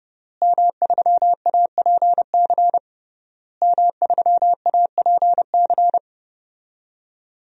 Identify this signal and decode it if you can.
Morse code, heard as a keyed tone, about 30 wpm, 700 Hz. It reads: M3APC M3APC
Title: DXCC-100 Call Signs - 5 Characters - Repeated 3x - 30wpm